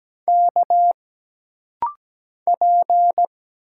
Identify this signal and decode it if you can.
Morse code, heard as a keyed tone, about 17 wpm, 700 Hz. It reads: K P
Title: Single Letters - Rapid-Fire 17wpm